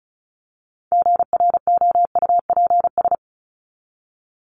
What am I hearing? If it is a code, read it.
Morse code, heard as a keyed tone, about 34 wpm, 700 Hz. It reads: GROUPS
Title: ES:S Suffix - Top 100 Words with Base Words 35wpm